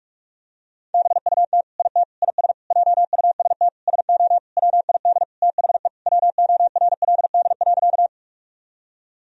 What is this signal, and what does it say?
Morse code, heard as a keyed tone, about 41 wpm, 700 Hz. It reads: BUT IT IS JUST SO WID THE WORLD.
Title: Sentences from Top 200 Words and Abbreviations 45wpm